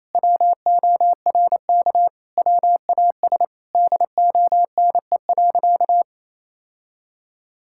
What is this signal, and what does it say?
Morse code, heard as a keyed tone, about 28 wpm, 700 Hz. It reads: WORK WAS DONE.
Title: Sentences from Top 1000 Words - Review Part 5 - 28wpm